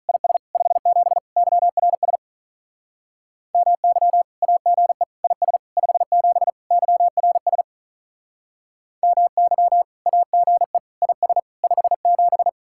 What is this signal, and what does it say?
Morse code, heard as a keyed tone, about 46 wpm, 700 Hz. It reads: IS 57 YRS MY AGE IS 57 YRS MY AGE IS 57